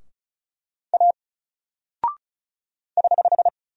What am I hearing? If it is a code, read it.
Morse code, heard as a keyed tone, about 36 wpm, 700 Hz. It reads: A <HH>
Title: Single Character (Letter:Number:Punctuation:Prosign) 35wpm